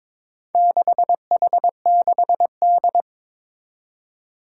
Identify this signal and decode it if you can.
Morse code, heard as a keyed tone, about 22 wpm, 700 Hz. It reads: 6H6D
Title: Four Challenging Letter-Number Combinations 22wpm